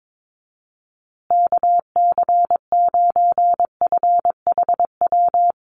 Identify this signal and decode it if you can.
Morse code, heard as a keyed tone, about 22 wpm, 700 Hz. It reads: KC9FHW
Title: USA Call Signs Repeated 3x 22wpm